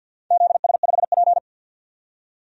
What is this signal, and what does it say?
Morse code, heard as a keyed tone, about 52 wpm, 700 Hz. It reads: ZS5P
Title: DXCC-100 Call Signs - 4 Characters - 50wpm